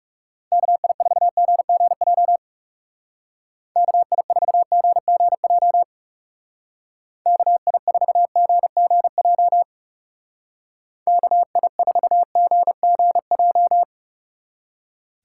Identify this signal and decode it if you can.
Morse code, heard as a keyed tone, about 38 wpm, 700 Hz. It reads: KI4GGJ KI4GGJ KI4GGJ KI4GGJ